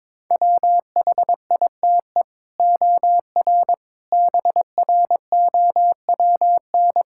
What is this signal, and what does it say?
Morse code, heard as a keyed tone, about 21 wpm, 700 Hz. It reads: WHITE OR BROWN
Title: Sentences from Top 1600 Words 22wpm